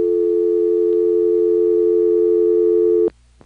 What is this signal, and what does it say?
Telephone call progress tone (North America), heard as a telephone network's tone, about 350, 440 Hz. Dial tone.